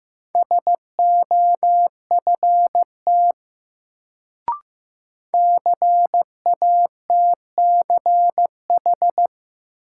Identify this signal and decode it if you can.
Morse code, heard as a keyed tone, about 15 wpm, 700 Hz. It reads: SOFT CATCH
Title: Sentences from Top 1500 Words - New Word Drill Alt - 15wpm